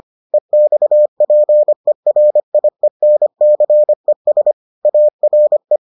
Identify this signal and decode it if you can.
Morse code, heard as a keyed tone, about 25 wpm, 600 Hz. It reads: EXPERIENCES ARE